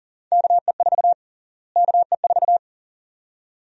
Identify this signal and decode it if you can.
Morse code, heard as a keyed tone, about 40 wpm, 700 Hz. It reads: KE4 KE4
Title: USA Call Sign Prefixes - Repeated 2x - 40wpm